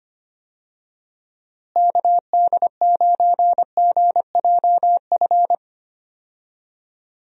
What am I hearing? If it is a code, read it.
Morse code, heard as a keyed tone, about 25 wpm, 700 Hz. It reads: KD9GJF
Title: USA Call Signs - 6 Characters - Repeated 2x - 25wpm